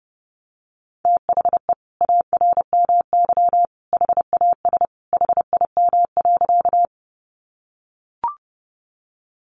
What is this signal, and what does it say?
Morse code, heard as a keyed tone, about 30 wpm, 700 Hz. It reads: THE ARMY HAS HIM.
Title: Sentences from Top 800 Words 30wpm